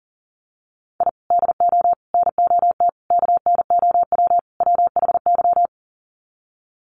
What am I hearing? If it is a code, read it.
Morse code, heard as a keyed tone, about 40 wpm, 700 Hz. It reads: I DO NOT KNOW WHY
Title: Sets of 5 Words 40wpm